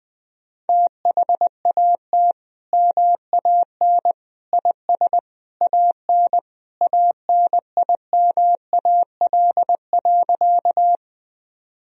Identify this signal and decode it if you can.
Morse code, heard as a keyed tone, about 20 wpm, 700 Hz. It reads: THAT MAN IS AN ANIMAL.
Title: Sentences from Top 1000 Words - Review Part 1 - 20wpm